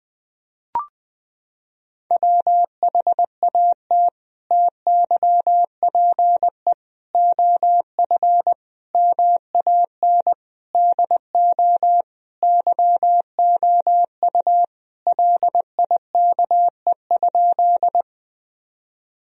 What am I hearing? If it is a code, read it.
Morse code, heard as a keyed tone, about 20 wpm, 700 Hz. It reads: WHAT TYPE OF MAN DO YOU LIKE?
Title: Sentences from Top 1000 Words - Review Part 3 - 20wpm